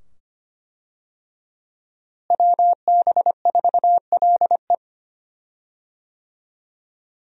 Morse code, heard as a keyed tone, about 25 wpm, 700 Hz. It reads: WB4LE
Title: USA Call Signs - 5 Characters - Repeated 2x - 25wpm